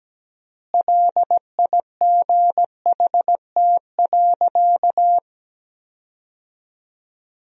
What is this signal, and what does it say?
Morse code, heard as a keyed tone, about 17 wpm, 700 Hz. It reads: LIGHT.
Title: 3x Word-Spacing - Sentences from Top 400 Words - 17wpm